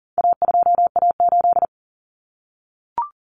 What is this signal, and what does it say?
Morse code, heard as a keyed tone, about 40 wpm, 700 Hz. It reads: A2A8
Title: Common Cut Numbers - 4-Digit 40wpm